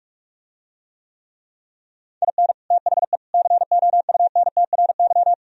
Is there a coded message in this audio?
Yes: Morse code from 2.2 to 5.3 s, heard as a keyed tone, about 44 wpm, 700 Hz. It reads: IN THE COUNTRY